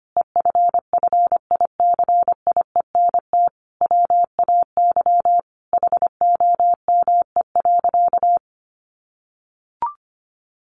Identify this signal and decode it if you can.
Morse code, heard as a keyed tone, about 25 wpm, 700 Hz. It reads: EFFICIENT WAY HOME.
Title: Sentences from Top 1300 Words 25wpm